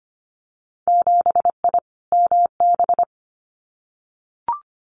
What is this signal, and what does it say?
Morse code, heard as a keyed tone, about 25 wpm, 700 Hz. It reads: 7I MB
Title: Winter Field Day Exchanges Encore 25wpm